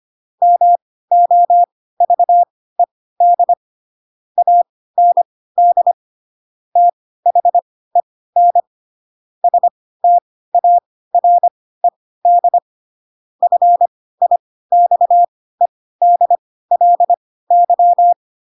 Morse code, heard as a keyed tone, about 25 wpm, spaced as for 16 wpm, 700 Hz. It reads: MOVED AND THEN STARED FIXEDLY
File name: Pinocchio / Pinocchio ch3 10 25 ik1hge